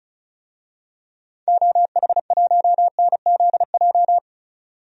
Morse code, heard as a keyed tone, about 36 wpm, 700 Hz. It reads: OH1NZJ
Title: DXCC-100 Call Signs - 6 Characters - 35wpm